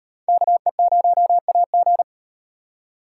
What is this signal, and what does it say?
Morse code, heard as a keyed tone, about 38 wpm, 700 Hz. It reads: KE0AG